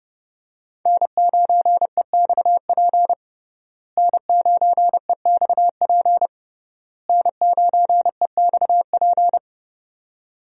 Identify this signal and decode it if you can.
Morse code, heard as a keyed tone, about 30 wpm, 700 Hz. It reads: N9EXP N9EXP N9EXP
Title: USA Call Signs - 5 Characters - Repeated 3x - 30wpm